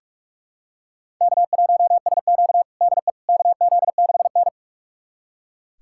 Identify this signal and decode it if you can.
Morse code, heard as a keyed tone, about 42 wpm, 700 Hz. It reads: K1SQ DE KZ6N
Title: USA Callsign DE Callsign - 4 Characters - 45wpm